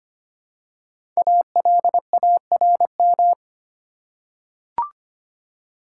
Morse code, heard as a keyed tone, about 25 wpm, 700 Hz. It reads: ALARM
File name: 5 Letter Words - First 3 Letters Given - 25wpm